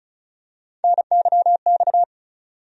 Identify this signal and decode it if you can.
Morse code, heard as a keyed tone, about 36 wpm, 700 Hz. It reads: NYX
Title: USA Call Sign Postfixes - 3 Letters 35wpm